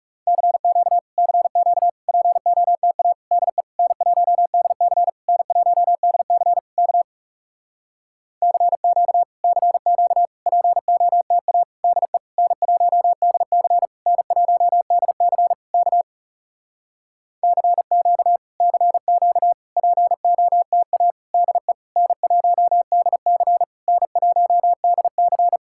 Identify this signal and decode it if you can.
Morse code, heard as a keyed tone, about 40 wpm, 700 Hz. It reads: CQ CQ POTA DE N1DC N1DC K CQ CQ POTA DE N1DC N1DC K CQ CQ POTA DE N1DC N1DC